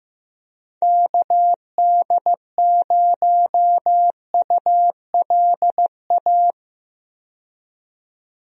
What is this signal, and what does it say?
Morse code, heard as a keyed tone, about 15 wpm, 700 Hz. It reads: KD0ULA
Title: USA Call Signs - 6 Characters - 15wpm